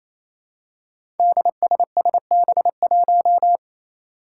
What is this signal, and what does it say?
Morse code, heard as a keyed tone, about 28 wpm, 700 Hz. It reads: DSSB1
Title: Five Challenging Letter-Number Combinations 28wpm